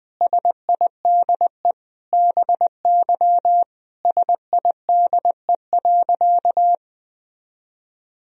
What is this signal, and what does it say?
Morse code, heard as a keyed tone, about 20 wpm, 700 Hz. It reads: SIDE BY SIDE.
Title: Sentences from Top 1000 Words - Review Part 2 - 20wpm